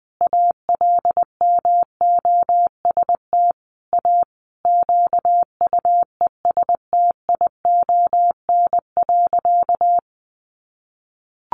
Morse code, heard as a keyed tone, about 20 wpm, 700 Hz. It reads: ALMOST A QUESTION.
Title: Sentences from Top 1200 Words 20wpm